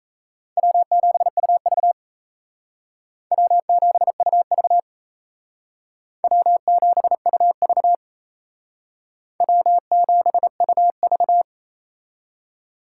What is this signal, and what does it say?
Morse code, heard as a keyed tone, about 36 wpm, 700 Hz. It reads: W7UV W7UV W7UV W7UV